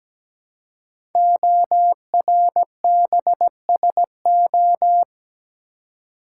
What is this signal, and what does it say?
Morse code, heard as a keyed tone, about 17 wpm, 700 Hz. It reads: ORBSO
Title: Code Groups - 5 Letter:Numbers - 17wpm